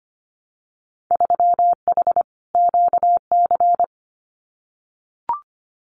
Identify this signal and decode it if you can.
Morse code, heard as a keyed tone, about 25 wpm, 700 Hz. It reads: 3H QC
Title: Winter Field Day Exchanges Encore 25wpm